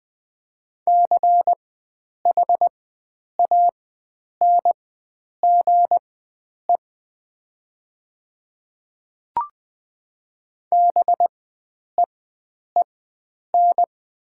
Morse code, heard as a keyed tone, about 20 wpm, spaced as for 9 wpm, 700 Hz. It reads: CHANGE BEEN
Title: 4x Character Spacing - Top 500 Words - 20wpm